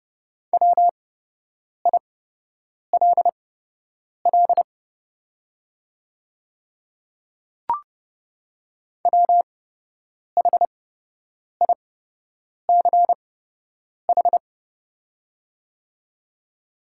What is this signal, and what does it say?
Morse code, heard as a keyed tone, about 30 wpm, spaced as for 8 wpm, 700 Hz. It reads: WILL WHICH
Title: 8x Character Spacing - Top 500 Words - 30wpm